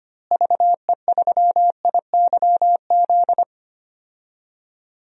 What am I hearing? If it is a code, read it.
Morse code, heard as a keyed tone, about 25 wpm, 700 Hz. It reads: VE3IYZ
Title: DXCC-100 Call Signs - 6 Characters - Repeated 3x - 25wpm